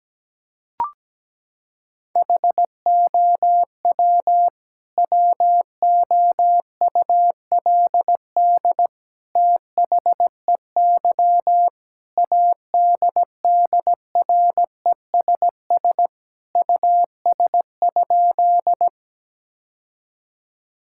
Morse code, heard as a keyed tone, about 17 wpm, 700 Hz. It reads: HOW WOULD THEY ADDRESS US?
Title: Sentences from Top 1000 Words - Review Part 6 - 17wpm